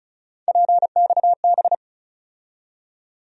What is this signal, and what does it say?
Morse code, heard as a keyed tone, about 34 wpm, 700 Hz. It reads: PXB
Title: USA Call Sign Postfixes - 3 Letters 35wpm